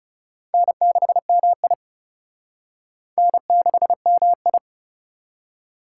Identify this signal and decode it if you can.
Morse code, heard as a keyed tone, about 34 wpm, 700 Hz. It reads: N6MI N6MI
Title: Speed Racing - USA 1x2 Call Signs 30wpm